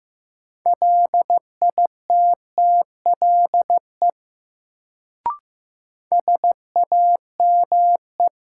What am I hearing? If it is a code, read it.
Morse code, heard as a keyed tone, about 15 wpm, 700 Hz. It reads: LITTLE SAME
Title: Sentences from Top 200 Words - New Word Drill - 15wpm